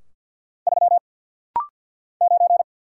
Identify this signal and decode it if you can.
Morse code, heard as a keyed tone, about 48 wpm, 700 Hz. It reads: 3 9
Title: Single Letter-Number - Rapid-Fire 50wpm